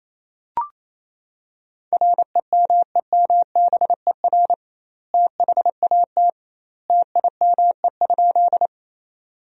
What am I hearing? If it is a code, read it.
Morse code, heard as a keyed tone, about 28 wpm, 700 Hz. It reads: REMEMBER THAT TIME?
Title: 2x Word-Spacing - Sentences from Top 400 Words - 28wpm